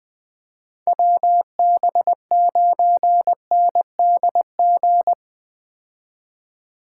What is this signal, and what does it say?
Morse code, heard as a keyed tone, about 20 wpm, 700 Hz. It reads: WB9NDG
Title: 2023 POTA Hunter Call Signs - 20wpm